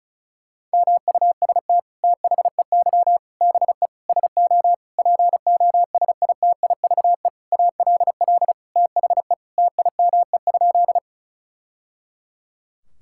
Morse code, heard as a keyed tone, about 36 wpm, 700 Hz. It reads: MUST THEY BE SO POSITIVE ALL THE TIME?
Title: Sentences from Top 1000 Words - Review Part 5 - 35wpm